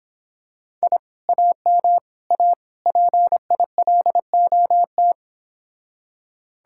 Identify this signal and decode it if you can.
Morse code, heard as a keyed tone, about 26 wpm, 700 Hz. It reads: I AM A PILOT